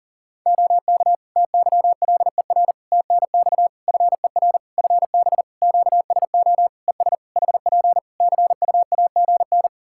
Morse code, heard as a keyed tone, about 40 wpm, 700 Hz. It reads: OK TYLER TNX FER FB QSO ES HP CUAGN
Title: Practice QSO - Exchange 4 - Ending - 40wpm